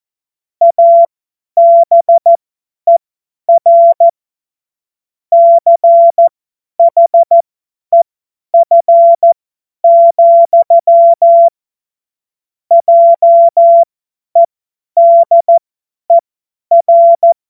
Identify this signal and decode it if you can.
Morse code, heard as a keyed tone, about 14 wpm, spaced as for 10 wpm, 680 Hz. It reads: ABER CHEF, JEDER
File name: Morsekurs / Witze / Witz 06. Dezember 2022 50 F70